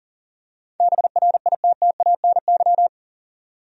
Morse code, heard as a keyed tone, about 40 wpm, 700 Hz. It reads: BRITTANY
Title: USA Common Names - 40wpm